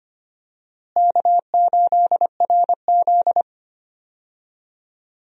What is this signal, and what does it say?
Morse code, heard as a keyed tone, about 25 wpm, 700 Hz. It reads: K8RZ
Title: USA Call Signs - 4 Characters - Repeated 2x - 25wpm